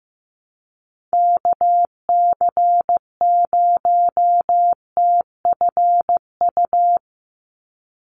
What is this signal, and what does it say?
Morse code, heard as a keyed tone, about 15 wpm, 700 Hz. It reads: KC0TFU
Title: USA Call Signs 15wpm